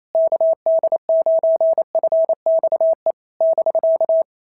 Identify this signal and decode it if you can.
Morse code, heard as a keyed tone, about 27 wpm, 650 Hz. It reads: KD9FXE <BK>
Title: SOTA QSO Practice 28wpm